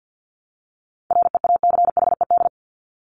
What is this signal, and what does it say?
Morse code, heard as a keyed tone, about 50 wpm, 700 Hz. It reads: REACHED
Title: ED Suffix - Top 100 Words with Base Words 50wpm